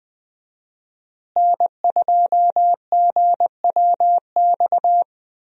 Morse code, heard as a keyed tone, about 20 wpm, 700 Hz. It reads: N2GWX
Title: USA Call Signs - 5 Characters - Repeated 2x - 20wpm